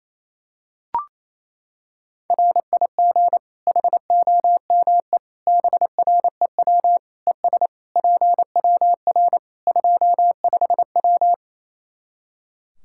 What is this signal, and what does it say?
Morse code, heard as a keyed tone, about 28 wpm, 700 Hz. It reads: RIG HOME BREW ES PWR 25W
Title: Practice QSO - Exchange 2 - Info - 28wpm